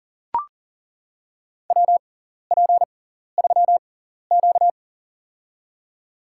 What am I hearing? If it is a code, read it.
Morse code, heard as a keyed tone, about 40 wpm, spaced as for 14 wpm, 700 Hz. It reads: WP3Q
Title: USA Call Signs - 4 Characters - 6x Character Spacing - 40wpm